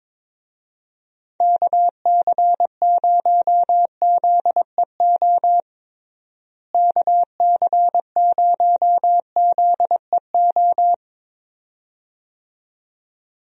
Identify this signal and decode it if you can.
Morse code, heard as a keyed tone, about 22 wpm, 700 Hz. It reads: KC0ZEO KC0ZEO
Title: USA Call Signs - 6 Characters - Repeated 2x - 22wpm